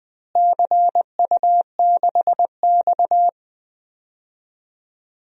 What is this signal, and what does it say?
Morse code, heard as a keyed tone, about 20 wpm, 700 Hz. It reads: CU6X
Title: DXCC-100 Call Signs - 4 Characters - Repeated 3x - 20wpm